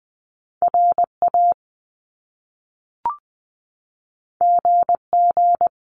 Morse code, heard as a keyed tone, about 20 wpm, 700 Hz. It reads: RA GG
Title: Top Two Letter Combinations in Words 20wpm